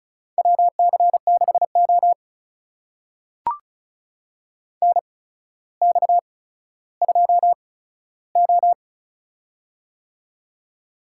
Morse code, heard as a keyed tone, about 36 wpm, spaced as for 15 wpm, 700 Hz. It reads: WC6O NX2O
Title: USA Call Signs - 4 Characters - 8x Character Spacing - 35wpm